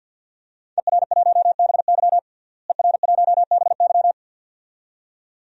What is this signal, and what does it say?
Morse code, heard as a keyed tone, about 46 wpm, 700 Hz. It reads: ER1BY ER1BY
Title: DXCC-100 Call Signs - 5 Characters - Repeated 2x - 50wpm